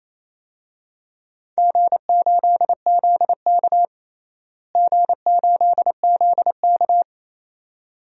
Morse code, heard as a keyed tone, about 28 wpm, 700 Hz. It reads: G8ZK G8ZK
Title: DXCC-100 Call Signs - 4 Characters - Repeated 2x - 28wpm